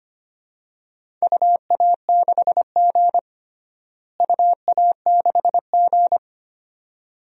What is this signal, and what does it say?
Morse code, heard as a keyed tone, about 25 wpm, 700 Hz. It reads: UA6G UA6G
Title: DXCC-100 Call Signs - 4 Characters - Repeated 2x - 25wpm